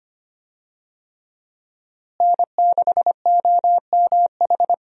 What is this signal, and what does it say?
Morse code, heard as a keyed tone, about 25 wpm, 700 Hz. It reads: N6OMH
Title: USA Call Signs - 5 Characters - Repeated 3x - 25wpm